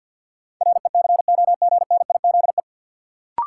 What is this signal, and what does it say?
Morse code, heard as a keyed tone, about 53 wpm, 700 Hz. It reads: RECOGNIZE
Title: Sentences from Top 1400 Words - New Word Drill - 50wpm